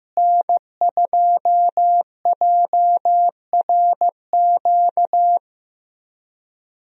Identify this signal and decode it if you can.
Morse code, heard as a keyed tone, about 15 wpm, 700 Hz. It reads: N2JRQ
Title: USA Call Signs - 5 Characters - Repeated 3x - 15wpm